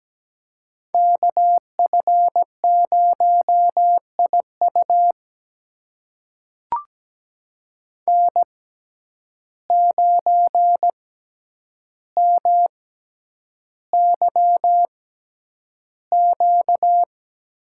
Morse code, heard as a keyed tone, about 17 wpm, spaced as for 9 wpm, 700 Hz. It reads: KF0IU N9MYQ
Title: USA Call Signs - 5 Characters - 6x Character Spacing - 17wpm